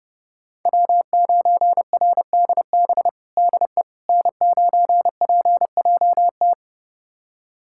Morse code, heard as a keyed tone, about 32 wpm, 700 Hz. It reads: W9RDB DE N9PJT
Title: USA Callsign DE Callsign - 5 Characters - 30wpm